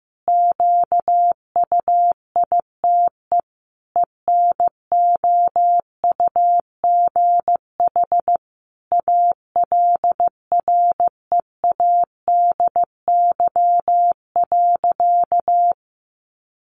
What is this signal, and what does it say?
Morse code, heard as a keyed tone, about 15 wpm, 700 Hz. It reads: QUITE ENOUGH ALREADY.
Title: Sentences from Top 700 Words 15wpm